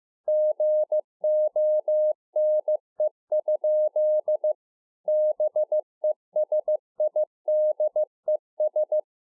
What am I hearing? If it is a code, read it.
Morse code, heard as a keyed tone, about 15 wpm, 610 Hz. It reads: GONE? BESIDES
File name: 15WPM